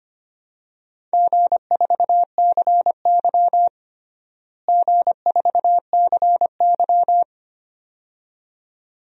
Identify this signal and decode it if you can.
Morse code, heard as a keyed tone, about 25 wpm, 700 Hz. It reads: G4CY G4CY
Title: DXCC-100 Call Signs - 4 Characters - Repeated 2x - 25wpm